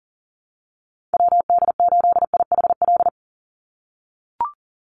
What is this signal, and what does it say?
Morse code, heard as a keyed tone, about 40 wpm, 700 Hz. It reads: WD8IHL
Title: USA Call Signs 40wpm